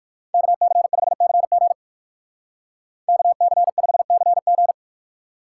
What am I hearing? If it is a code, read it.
Morse code, heard as a keyed tone, about 59 wpm, 700 Hz. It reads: KC5CG KC5CG